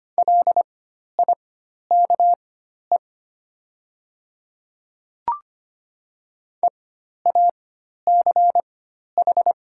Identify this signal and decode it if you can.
Morse code, heard as a keyed tone, about 25 wpm, spaced as for 12 wpm, 700 Hz. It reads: LIKE EACH
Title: 4x Character Spacing - Top 100 Words - 25wpm